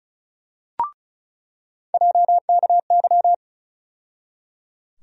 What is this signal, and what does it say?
Morse code, heard as a keyed tone, about 34 wpm, 700 Hz. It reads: JKY
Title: USA Call Sign Postfixes - 3 Letters 35wpm